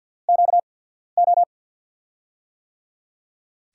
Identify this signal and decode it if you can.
Morse code, heard as a keyed tone, about 50 wpm, spaced as for 14 wpm, 700 Hz. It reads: QO
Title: USA Call Signs - 6 Characters - 8x Character Spacing - 50wpm